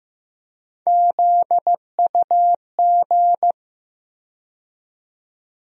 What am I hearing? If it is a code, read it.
Morse code, heard as a keyed tone, about 15 wpm, 700 Hz. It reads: ZUG
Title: USA Call Sign Postfixes - 3 Letters 15wpm